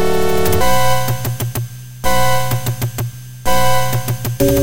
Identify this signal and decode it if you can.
There is no code.